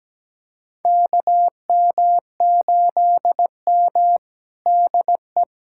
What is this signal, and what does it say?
Morse code, heard as a keyed tone, about 17 wpm, 700 Hz. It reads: KM8M DE
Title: USA Callsign DE Callsign - 4 Characters - 17wpm